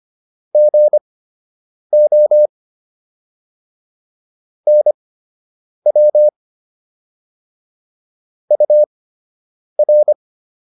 Morse code, heard as a keyed tone, about 25 wpm, spaced as for 8 wpm, 600 Hz. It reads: GO NW UR